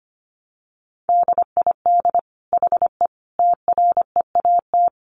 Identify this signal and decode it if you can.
Morse code, heard as a keyed tone, about 25 wpm, 700 Hz. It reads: DID HE TREAT
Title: Sentences from Top 1300 Words 25wpm